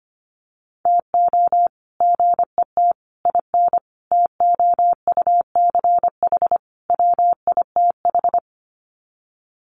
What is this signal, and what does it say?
Morse code, heard as a keyed tone, about 25 wpm, 700 Hz. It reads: TO GET IN TOUCH WITH
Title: Sets of 5 Words Encore2 25wpm